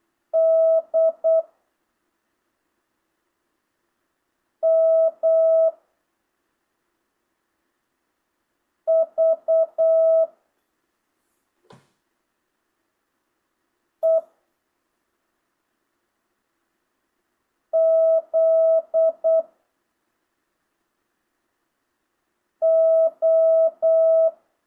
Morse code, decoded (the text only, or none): D M V E Z O